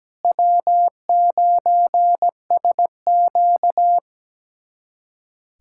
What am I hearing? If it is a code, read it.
Morse code, heard as a keyed tone, about 17 wpm, 700 Hz. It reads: W9SQ
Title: USA Callsign DE Callsign - 4 Characters - 17wpm